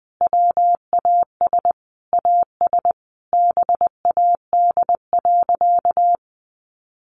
Morse code, heard as a keyed tone, about 20 wpm, 700 Hz. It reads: WAS AS BAD.
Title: Sentences from Top 1200 Words 20wpm